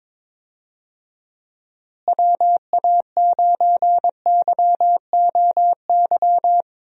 Morse code, heard as a keyed tone, about 22 wpm, 700 Hz. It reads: WA9YOY
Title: USA Call Signs - 6 Characters - Repeated 2x - 22wpm